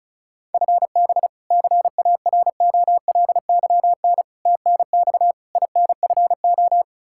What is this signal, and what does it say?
Morse code, heard as a keyed tone, about 36 wpm, 700 Hz. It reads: FB CAROLYN TNX INFO
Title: Practice QSO - Exchange 3 - More Info - 35wpm